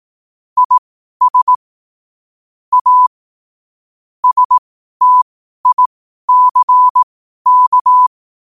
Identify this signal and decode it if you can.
Morse code, heard as a keyed tone, about 18 wpm, spaced as for 12 wpm, 1000 Hz. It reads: IS A STICK
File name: qotdfiles / 12wpm